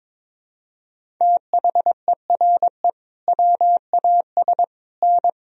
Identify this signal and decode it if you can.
Morse code, heard as a keyed tone, about 22 wpm, 700 Hz. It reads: THERE WAS N
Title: Sentences from Top 300 Words and Abbreviations 22wpm